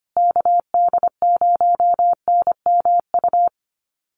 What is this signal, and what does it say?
Morse code, heard as a keyed tone, about 25 wpm, 700 Hz. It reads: KD0NMU
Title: USA Call Signs Repeated 2x 25wpm